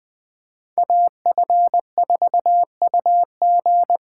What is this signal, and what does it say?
Morse code, heard as a keyed tone, about 20 wpm, 700 Hz. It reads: AF4UG
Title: Speed Racing - USA 2x2 Call Signs 17wpm